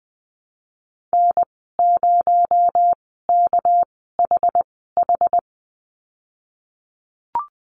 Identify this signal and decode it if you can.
Morse code, heard as a keyed tone, about 20 wpm, spaced as for 15 wpm, 700 Hz. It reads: N0KHH
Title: 231 Lesson 37g - Letter x - Call Signs 20wpm